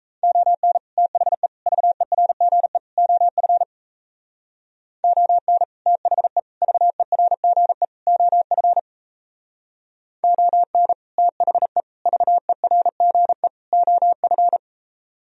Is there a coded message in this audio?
Yes: Morse code from 0.2 to 14.6 s, heard as a keyed tone, about 38 wpm, 700 Hz. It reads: ON THE VERGE OF ON THE VERGE OF ON THE VERGE OF